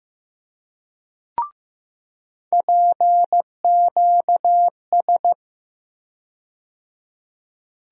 Morse code, heard as a keyed tone, about 15 wpm, 700 Hz. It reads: PQS
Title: USA Call Sign Postfixes - 3 Letters 15wpm